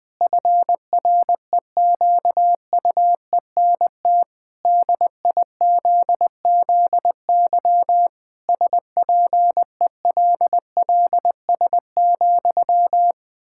Morse code, heard as a keyed tone, about 21 wpm, 700 Hz. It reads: FREQUENT DIZZY SPELLS,
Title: Pangram Challenge v2 - 20wpm